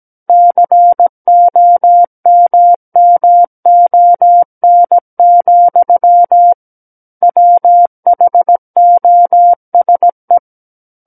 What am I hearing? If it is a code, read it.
Morse code, heard as a keyed tone, about 17 wpm, 700 Hz. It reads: COMMON, WHOSE
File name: Book One / 08-Chapter 8 - 17 WPM